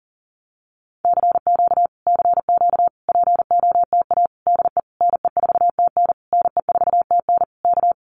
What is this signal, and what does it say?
Morse code, heard as a keyed tone, about 40 wpm, 700 Hz. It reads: CQ CQ POTA DE NE4TN NE4TN K
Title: Common QSO Phrases Encore 40wpm